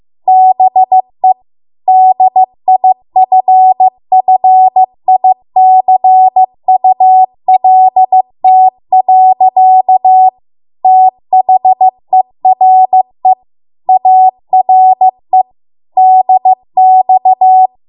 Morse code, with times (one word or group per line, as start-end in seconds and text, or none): 0.3-1.3 BE
1.9-10.3 DIFFICULT.
10.8-13.3 THERE
13.9-15.4 ARE
16.0-17.7 DX